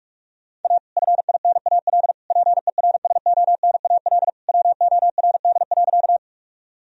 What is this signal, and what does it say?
Morse code, heard as a keyed tone, about 46 wpm, 700 Hz. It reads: A FINAL PERSONAL WORD.
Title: Sentences from Top 1000 Words - Review Part 4 - 45wpm